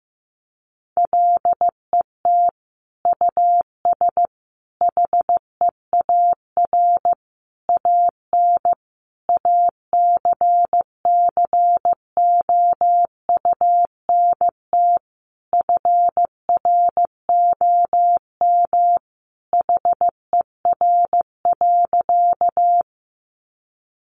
Morse code, heard as a keyed tone, about 15 wpm, 700 Hz. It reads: LET US HEAR AN ACCOUNT FROM HER.
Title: Sentences from Top 900 Words 15wpm